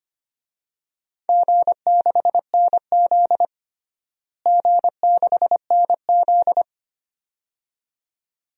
Morse code, heard as a keyed tone, about 25 wpm, 700 Hz. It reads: G6NZ G6NZ
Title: DXCC-100 Call Signs - 4 Characters - Repeated 2x - 25wpm